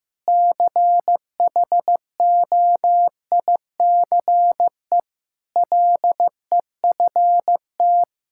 Morse code, heard as a keyed tone, about 15 wpm, 700 Hz. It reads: CHOICE LEFT
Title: Sentences from Top 1000 Words - Review Part 6 - 15wpm